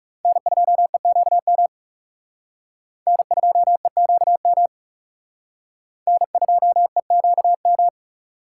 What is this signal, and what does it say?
Morse code, heard as a keyed tone, about 37 wpm, 700 Hz. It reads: N2EQM N2EQM N2EQM
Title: Speed Racing - USA 1x3 Call Signs 30wpm